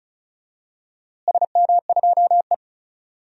Morse code, heard as a keyed tone, about 34 wpm, 690 Hz. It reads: SM2E
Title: DXCC-100 Call Signs - 4 Characters - Repeated 2x - 35wpm